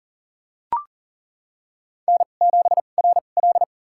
Morse code, heard as a keyed tone, about 37 wpm, spaced as for 28 wpm, 700 Hz. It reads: N7RL
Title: USA Call Signs - 2x Character Spacing - 40wpm